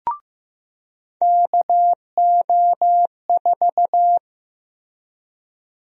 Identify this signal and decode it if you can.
Morse code, heard as a keyed tone, about 15 wpm, 700 Hz. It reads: KO4
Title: USA Call Sign Prefixes - Repeated 2x - 15wpm